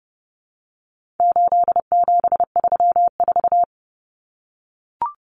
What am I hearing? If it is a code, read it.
Morse code, heard as a keyed tone, about 30 wpm, 700 Hz. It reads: 8734
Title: Common Cut Numbers - 4-Digit 30wpm